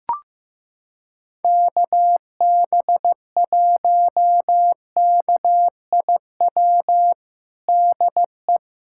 Morse code, heard as a keyed tone, about 15 wpm, 700 Hz. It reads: KB1KIW DE
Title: USA Callsign DE Callsign 15wpm